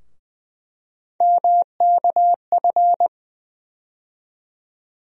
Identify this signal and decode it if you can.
Morse code, heard as a keyed tone, about 20 wpm, 700 Hz. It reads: MKF
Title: USA Call Sign Postfixes - 3 Letters 20wpm